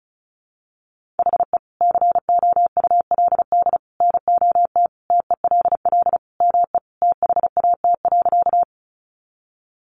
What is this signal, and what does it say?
Morse code, heard as a keyed tone, about 34 wpm, 700 Hz. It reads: HE COULD NOT TELL ME THAT.
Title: Sentences from Top 600 Words 35wpm